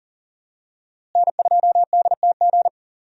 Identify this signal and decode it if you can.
Morse code, heard as a keyed tone, about 40 wpm, 700 Hz. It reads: N2DTG
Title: USA Call Signs - 5 Characters - Repeated 3x - 40wpm